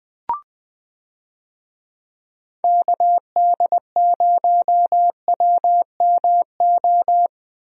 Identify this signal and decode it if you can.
Morse code, heard as a keyed tone, about 20 wpm, 700 Hz. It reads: KD0WMO
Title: USA Call Signs - 6 Characters - Repeated 3x - 20wpm